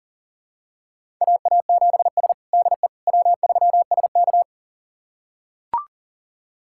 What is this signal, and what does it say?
Morse code, heard as a keyed tone, about 40 wpm, 700 Hz. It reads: AA7S DE W3SK
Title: USA Callsign DE Callsign - 4 Characters - 40wpm